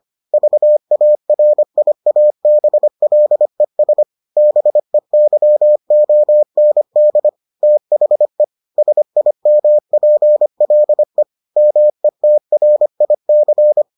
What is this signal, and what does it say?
Morse code, heard as a keyed tone, about 25 wpm, 600 Hz. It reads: VARIABLES BEYOND THE SIMPLE METRIC